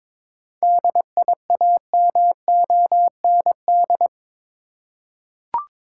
Morse code, heard as a keyed tone, about 21 wpm, 700 Hz. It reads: DIAMOND
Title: Seven Letter Words - 22wpm